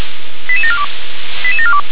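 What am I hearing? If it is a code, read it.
Five-tone selective call (ZVEI), heard as quick runs of five tones, tones of about 70 ms. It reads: 88742 88742
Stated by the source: ZVEI five-tone selective call received with an SDR.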